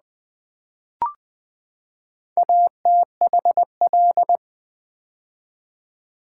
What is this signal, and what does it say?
Morse code, heard as a keyed tone, about 20 wpm, 700 Hz. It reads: ATHL
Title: Build A Word - 4 Letters At A Time - 8-Letter Words - 20wpm